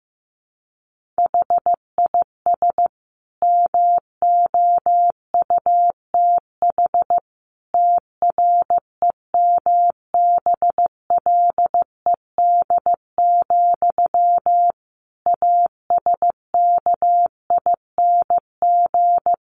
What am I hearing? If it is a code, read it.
Morse code, heard as a keyed tone, about 15 wpm, 700 Hz. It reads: HIS MOUTH TREMBLED, ASKING
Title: A Sound Of Thunder 15wpm